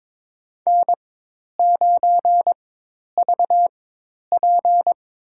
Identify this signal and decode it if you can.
Morse code, heard as a keyed tone, about 22 wpm, spaced as for 10 wpm, 700 Hz. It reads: N9VP
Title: USA Call Signs - 4 Characters - 4x Character Spacing - 22wpm